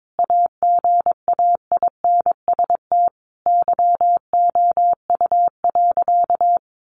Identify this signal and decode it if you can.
Morse code, heard as a keyed tone, about 22 wpm, 700 Hz. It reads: AGAINST YOU.
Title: Sentences from Top 700 Words 22wpm